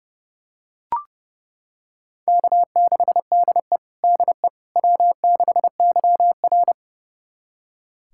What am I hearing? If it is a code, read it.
Morse code, heard as a keyed tone, about 32 wpm, 700 Hz. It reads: K6DE DE W6YR
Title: USA Callsign DE Callsign - 4 Characters - 30wpm